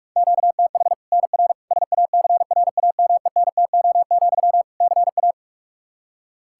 Morse code, heard as a keyed tone, about 46 wpm, 700 Hz. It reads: QTH NR SACRAMENTO, CA
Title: Practice QSO - Exchange 1 - Beginning - 45wpm